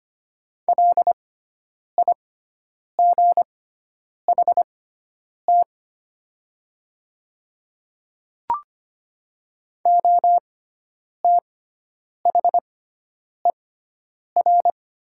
Morse code, heard as a keyed tone, about 25 wpm, spaced as for 9 wpm, 700 Hz. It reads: LIGHT OTHER
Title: 6x Character Spacing - Top 200 Words - 25wpm